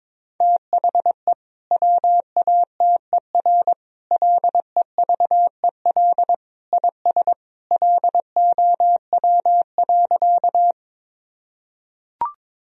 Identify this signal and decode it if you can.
Morse code, heard as a keyed tone, about 22 wpm, 700 Hz. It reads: THE WATER LEVEL IS LOW.
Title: Sentences from Top 1000 Words - Review Part 4 - 22wpm